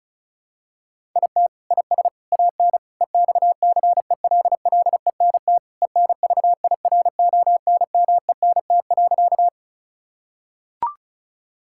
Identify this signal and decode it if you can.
Morse code, heard as a keyed tone, about 36 wpm, 700 Hz. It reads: IT IS AN EXCELLENT ENVIRONMENT.
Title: Sentences from Top 1000 Words - Review Part 5 - 35wpm